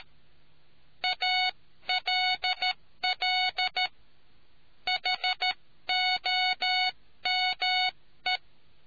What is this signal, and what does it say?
Morse code, heard as a keyed tone, about 13 wpm, 740 Hz. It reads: ALL HOME